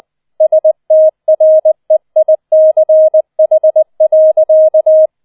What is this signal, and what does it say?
Morse code, heard as a keyed tone, about 20 wpm, 630 Hz. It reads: STREICH.